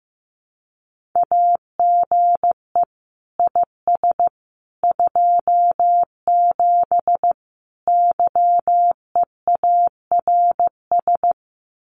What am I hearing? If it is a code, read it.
Morse code, heard as a keyed tone, about 15 wpm, 700 Hz. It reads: AGE IS 27 YEARS
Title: Common QSO Phrases Encore 15wpm